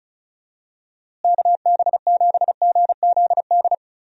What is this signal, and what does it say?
Morse code, heard as a keyed tone, about 34 wpm, 700 Hz. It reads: KB7GZD
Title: USA Call Signs - 6 Characters - Repeated 2x - 35wpm